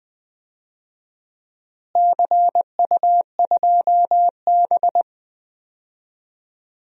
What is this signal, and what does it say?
Morse code, heard as a keyed tone, about 20 wpm, 700 Hz. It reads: CU2B
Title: DXCC-100 Call Signs - 4 Characters - Repeated 3x - 20wpm